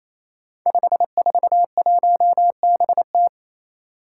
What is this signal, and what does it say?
Morse code, heard as a keyed tone, about 27 wpm, 700 Hz. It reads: 541BT
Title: Five Challenging Letter-Number Combinations 28wpm